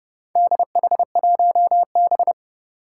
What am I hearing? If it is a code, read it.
Morse code, heard as a keyed tone, about 30 wpm, 700 Hz. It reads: DH1B
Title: Four Challenging Letter-Number Combinations 30wpm